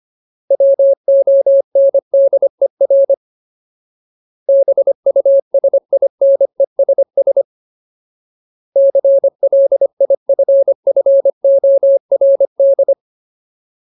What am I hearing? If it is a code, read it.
Morse code, heard as a keyed tone, about 25 wpm, 550 Hz. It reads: WONDER BUSINESS CLIFFORD
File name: Ultimate Word List / 25WPM Ultimate